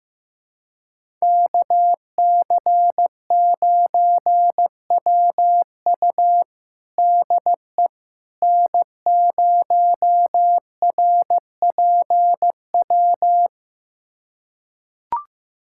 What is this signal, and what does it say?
Morse code, heard as a keyed tone, about 15 wpm, 700 Hz. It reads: KC9WU DE N0RPW
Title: USA Callsign DE Callsign - 5 Characters - 15wpm